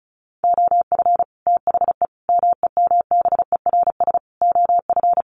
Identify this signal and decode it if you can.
Morse code, heard as a keyed tone, about 34 wpm, 700 Hz. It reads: OF THE MEMBERS OF
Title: Sets of 5 Words Encore2 35wpm